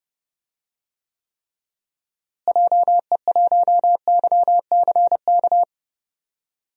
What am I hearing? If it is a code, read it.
Morse code, heard as a keyed tone, about 31 wpm, 700 Hz. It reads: JE1YCK
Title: DXCC-100 Call Signs - 6 Characters - Repeated 3x - 30wpm